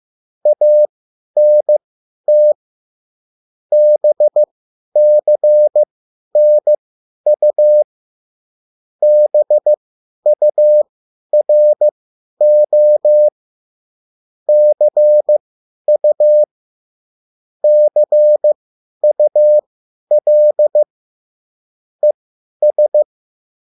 Morse code, heard as a keyed tone, about 15 wpm, spaced as for 10 wpm, 600 Hz. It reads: ANT BCNU BURO CU CUL ES